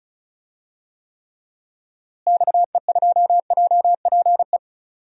Morse code, heard as a keyed tone, about 34 wpm, 700 Hz. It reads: XE2JPE
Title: DXCC-100 Call Signs - 6 Characters - Repeated 2x - 35wpm